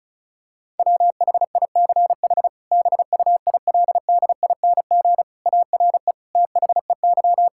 Morse code, heard as a keyed tone, about 34 wpm, 700 Hz. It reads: WHICH BUILDING ARE THEY
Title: Sentences from Top 1000 Words - Review Part 5 - 35wpm